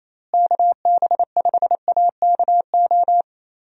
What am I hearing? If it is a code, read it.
Morse code, heard as a keyed tone, about 27 wpm, 700 Hz. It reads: KB5AKO
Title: USA Call Signs - 6 Characters - Repeated 2x - 28wpm